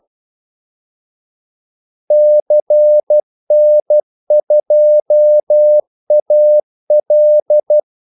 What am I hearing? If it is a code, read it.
Morse code, heard as a keyed tone, about 12 wpm, 600 Hz. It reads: CN2AL
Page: 2 ekstra ordmellemrum mellem kaldesignalerne